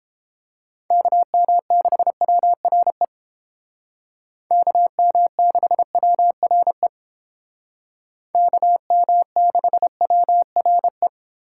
Morse code, heard as a keyed tone, about 30 wpm, 700 Hz. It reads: KM6WRE KM6WRE KM6WRE